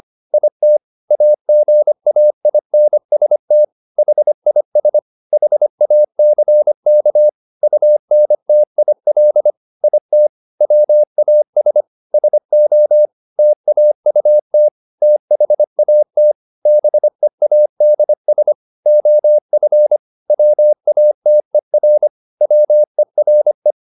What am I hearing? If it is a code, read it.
Morse code, heard as a keyed tone, about 25 wpm, 600 Hz. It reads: IT AGAINST HIS HACK UNTIL IT WAS SO TAUT THAT BEADS OF WATER WERE
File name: the old man and the sea part 1 25wpm